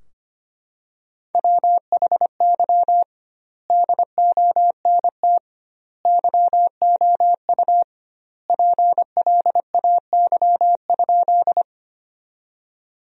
Morse code, heard as a keyed tone, about 25 wpm, 700 Hz. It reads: WHY DONT YOU PLAY?
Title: 2x Word-Spacing - Sentences from Top 300 Words - 25wpm